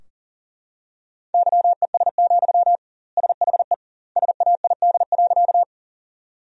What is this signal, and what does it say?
Morse code, heard as a keyed tone, about 40 wpm, 700 Hz. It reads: YES, SHE SAID.
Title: 2x Word-Spacing - Sentences from Top 500 Words - 40wpm